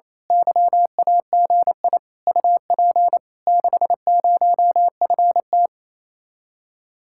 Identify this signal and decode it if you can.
Morse code, heard as a keyed tone, about 27 wpm, 700 Hz. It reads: YAGI UP 60FT
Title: Speed-Racing Common QSO Phrases Encore - 28wpm